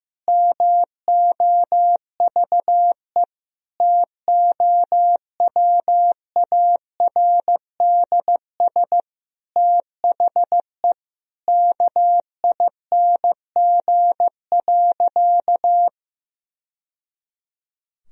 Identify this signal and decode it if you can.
Morse code, heard as a keyed tone, about 16 wpm, 700 Hz. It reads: MOVE TOWARDS THE KING.
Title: Sentences from Top 1000 Words - Review Part 2 - 15wpm